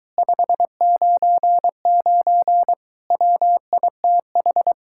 Morse code, heard as a keyed tone, about 22 wpm, 700 Hz. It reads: 599 WITH